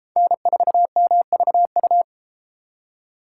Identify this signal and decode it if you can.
Morse code, heard as a keyed tone, about 32 wpm, 700 Hz. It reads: N4MVU